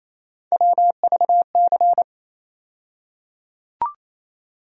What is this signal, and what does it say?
Morse code, heard as a keyed tone, about 28 wpm, 700 Hz. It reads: WVC
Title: USA Call Sign Postfixes - 3 Letters 28wpm